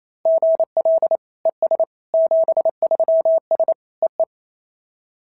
Morse code, heard as a keyed tone, about 28 wpm, 690 Hz. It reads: GL ES 73S EE
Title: SOTA QSO Practice 28wpm